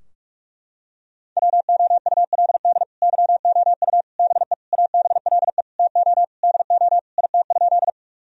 Morse code, heard as a keyed tone, about 41 wpm, 700 Hz. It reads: WOULD YOU BE ABLE TO DO IT?
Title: Sentences from Top 1000 Words - Review Part 2 - 45wpm